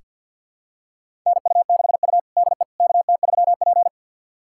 Morse code, heard as a keyed tone, about 46 wpm, 700 Hz. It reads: NU6U DE KT3P
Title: USA Callsign DE Callsign - 4 Characters - 50wpm